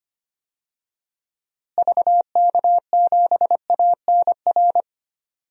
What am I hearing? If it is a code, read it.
Morse code, heard as a keyed tone, about 25 wpm, 700 Hz. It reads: VK7ANR
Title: DXCC-100 Call Signs - 6 Characters - Repeated 2x - 25wpm